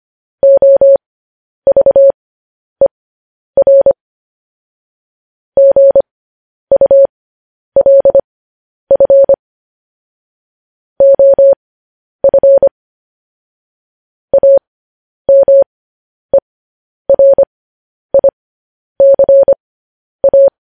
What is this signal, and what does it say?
Morse code, heard as a keyed tone, about 25 wpm, spaced as for 10 wpm, 550 Hz. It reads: OVER GULF OF AMERICA
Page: news-headlines-in-morse-code-at-10-wpm